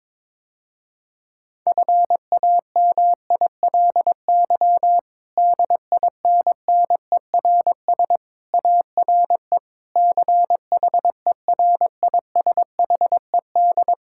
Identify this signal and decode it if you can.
Morse code, heard as a keyed tone, about 22 wpm, 700 Hz. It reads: FAMILY DINNERS ARE CHERISHED
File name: Difficult Sentences with Hints - 22wpm